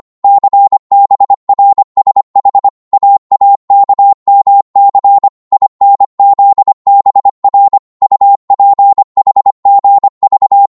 Morse code, heard as a keyed tone, about 25 wpm, 800 Hz. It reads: CBRSH AAKMC INZBR UPHGV